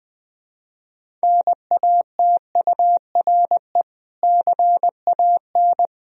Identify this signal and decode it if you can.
Morse code, heard as a keyed tone, about 21 wpm, 700 Hz. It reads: NATURE CAN
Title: Sentences from Top 1000 Words - Review Part 3 - 20wpm